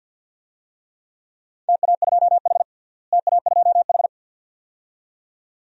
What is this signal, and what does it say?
Morse code, heard as a keyed tone, about 52 wpm, 700 Hz. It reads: TA2H TA2H
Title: DXCC-100 Call Signs - 4 Characters - Repeated 2x - 50wpm